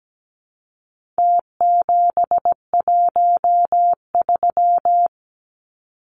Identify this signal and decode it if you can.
Morse code, heard as a keyed tone, about 17 wpm, 700 Hz. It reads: T713
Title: Numbers - Contest Style - 4-Digit 17wpm